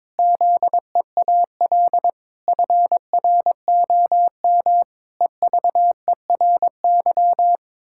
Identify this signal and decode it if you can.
Morse code, heard as a keyed tone, about 22 wpm, 700 Hz. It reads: ZEAL FROM EVERY
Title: Pangram Challenge v2 - 22wpm